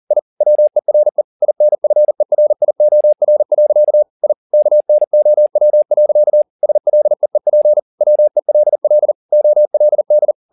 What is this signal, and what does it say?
Morse code, heard as a keyed tone, about 40 wpm, 600 Hz. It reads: I WERE INFERIOR. I KNOW. SLEEP WELL OLD